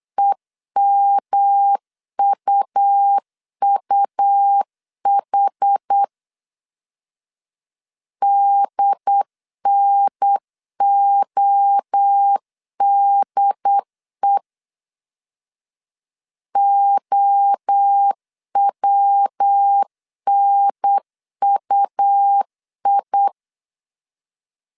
Morse code, heard as a keyed tone, about 8 wpm, 800 Hz. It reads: EMUUH DNODE OWNUI
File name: CW / DET / 1 3 1500C S B 8wpm 4141